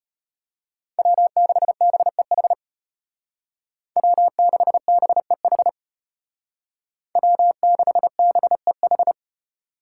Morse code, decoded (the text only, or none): W6BEH W6BEH W6BEH